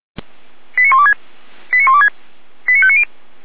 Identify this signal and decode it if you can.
Five-tone selective call (ZVEI), heard as quick runs of five tones, tones of about 70 ms. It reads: 89126 89126 89590